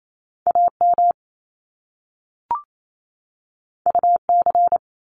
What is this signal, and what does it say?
Morse code, heard as a keyed tone, about 28 wpm, 700 Hz. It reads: AM UC
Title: Top Two Letter Combinations in Words 28wpm